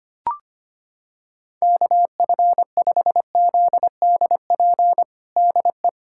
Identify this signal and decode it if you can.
Morse code, heard as a keyed tone, about 25 wpm, 700 Hz. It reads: KF5ZDP DE
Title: USA Callsign DE Callsign 25wpm